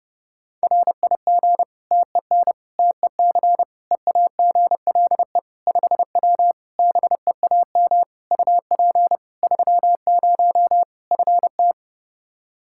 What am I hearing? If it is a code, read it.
Morse code, heard as a keyed tone, about 30 wpm, 700 Hz. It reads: RIG TEN TEC EAGLE 5W BEAM UP 30 FT
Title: Practice QSO - Exchange 2 - Info - 30wpm